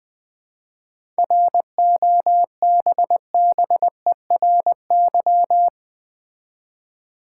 Morse code, heard as a keyed tone, about 20 wpm, 700 Hz. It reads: ROBBERY
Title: 7 Letter Words - First 3 Letters Given - 20wpm